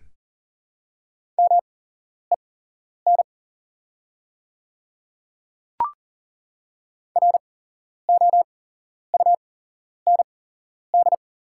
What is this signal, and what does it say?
Morse code, heard as a keyed tone, about 40 wpm, spaced as for 11 wpm, 700 Hz. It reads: MEN ROUND
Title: 8x Character Spacing - Top 500 Words - 40wpm